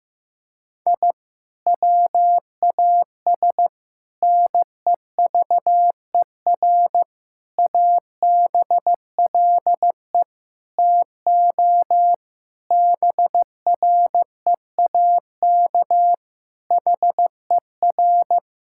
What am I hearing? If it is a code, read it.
Morse code, heard as a keyed tone, about 16 wpm, 700 Hz. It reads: I WAS NEVER ABLE TO BREAK HER
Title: Sentences from Top 1000 Words - Review Part 6 - 15wpm